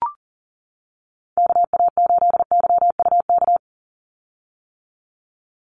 Morse code, heard as a keyed tone, about 40 wpm, 700 Hz. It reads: KA8YUK
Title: USA Call Signs 40wpm